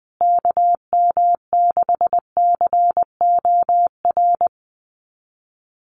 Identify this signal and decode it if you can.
Morse code, heard as a keyed tone, about 19 wpm, 700 Hz. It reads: KM6COR
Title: USA Call Signs Repeated 3x 20wpm